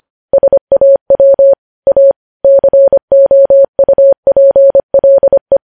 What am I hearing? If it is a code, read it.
Morse code, heard as a keyed tone, about 25 wpm, 550 Hz. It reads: SAW A COUPLE
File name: podcast / News headlines-1x-25x25